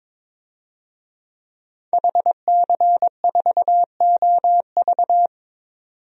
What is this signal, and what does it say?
Morse code, heard as a keyed tone, about 22 wpm, 700 Hz. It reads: HC4OV
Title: DXCC-100 Call Signs - 5 Characters - 22wpm